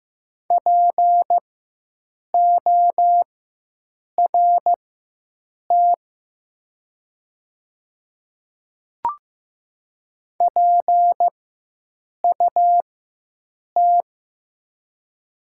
Morse code, heard as a keyed tone, about 15 wpm, spaced as for 7 wpm, 700 Hz. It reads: PORT PUT
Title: 4x Character Spacing - Top 200 Words - 15wpm